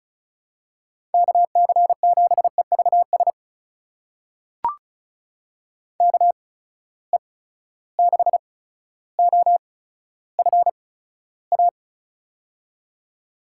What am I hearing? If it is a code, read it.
Morse code, heard as a keyed tone, about 36 wpm, spaced as for 15 wpm, 700 Hz. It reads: KC7EVS KE6OFA
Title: USA Call Signs - 6 Characters - 8x Character Spacing - 35wpm